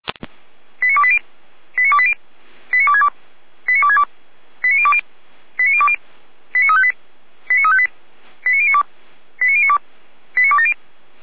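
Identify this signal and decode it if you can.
Five-tone selective call (ZVEI), heard as quick runs of five tones, tones of about 70 ms. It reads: 89280 89280 89262 89262 89020 89020 89358 89358 89092 89092 89270
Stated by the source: ZVEI five-tone selective call received on SDR.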